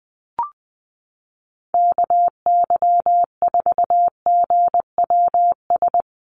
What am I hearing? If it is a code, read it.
Morse code, heard as a keyed tone, about 20 wpm, 700 Hz. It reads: KY4GWS
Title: USA Call Signs Repeated 3x 20wpm